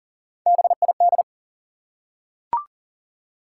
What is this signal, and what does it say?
Morse code, heard as a keyed tone, about 43 wpm, 700 Hz. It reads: BID
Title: Three Letter Words - 40wpm